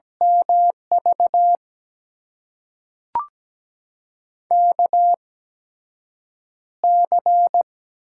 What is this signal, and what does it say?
Morse code, heard as a keyed tone, about 17 wpm, spaced as for 7 wpm, 700 Hz. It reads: MV KC
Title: USA Call Signs - 6 Characters - 8x Character Spacing - 17wpm